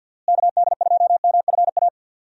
Morse code, heard as a keyed tone, about 51 wpm, 700 Hz. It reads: KD2MFA
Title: USA Call Signs - 6 Characters - Repeated 3x - 50wpm